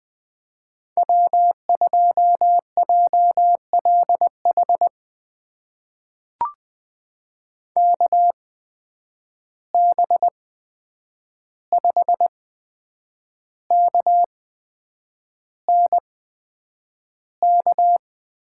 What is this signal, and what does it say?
Morse code, heard as a keyed tone, about 20 wpm, spaced as for 8 wpm, 700 Hz. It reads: W2JLH KB5KNK
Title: USA Call Signs - 8x Character Spacing - 20wpm